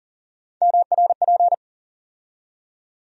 Morse code, heard as a keyed tone, about 40 wpm, 700 Hz. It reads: MRP
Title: USA Call Sign Postfixes - 3 Letters 40wpm